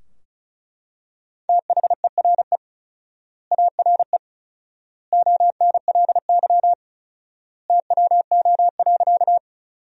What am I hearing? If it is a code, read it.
Morse code, heard as a keyed tone, about 34 wpm, 700 Hz. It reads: THERE ARE ONLY TWO.
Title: 4x Word-Spacing - Sentences from Top 200 Words - 35wpm